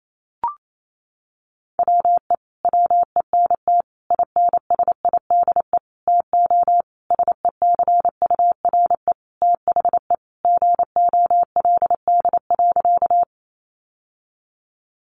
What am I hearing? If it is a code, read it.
Morse code, heard as a keyed tone, about 28 wpm, 700 Hz. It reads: WE WENT INSIDE TO SECURE THE GOLD.
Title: Sentences from Top 1200 Words 28wpm